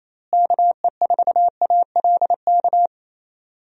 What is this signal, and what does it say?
Morse code, heard as a keyed tone, about 27 wpm, 700 Hz. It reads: KE4ALK
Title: USA Call Signs - 6 Characters - Repeated 3x - 28wpm